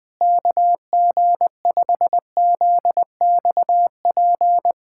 Morse code, heard as a keyed tone, about 20 wpm, 700 Hz. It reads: KG5ZXP
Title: USA Callsign DE Callsign - 6 Characters - 20wpm